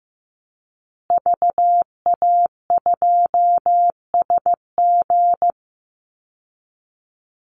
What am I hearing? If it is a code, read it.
Morse code, heard as a keyed tone, about 15 wpm, 700 Hz. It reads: VA2SG
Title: Call Signs Repeated 2x - 15wpm